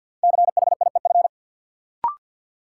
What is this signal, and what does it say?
Morse code, heard as a keyed tone, about 52 wpm, 700 Hz. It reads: CHIEF
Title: Sets of 5 Words Encore - New Word Drill - 50wpm